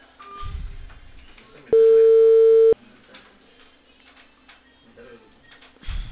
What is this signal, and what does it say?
Ringback tone (China, 450 Hz).